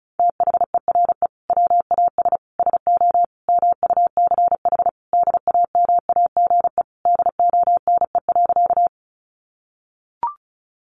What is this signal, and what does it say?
Morse code, heard as a keyed tone, about 34 wpm, 700 Hz. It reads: THERE WAS SO MUCH DAMAGE DONE.
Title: Sentences from Top 1100 Words 35wpm